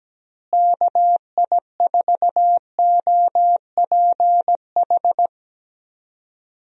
Morse code, heard as a keyed tone, about 17 wpm, 700 Hz. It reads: KI4OPH
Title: USA Call Signs - 6 Characters - Repeated 2x - 17wpm